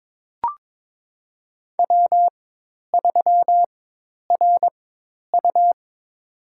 Morse code, heard as a keyed tone, about 21 wpm, spaced as for 10 wpm, 700 Hz. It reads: W3RU
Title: USA Call Signs - 4 Characters - 4x Character Spacing - 22wpm